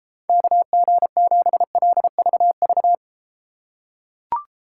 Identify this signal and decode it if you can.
Morse code, heard as a keyed tone, about 33 wpm, 700 Hz. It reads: KG7LVV